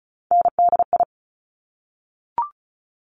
Morse code, heard as a keyed tone, about 36 wpm, 700 Hz. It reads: NDI
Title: Top Three Letter Combinations in Words 35wpm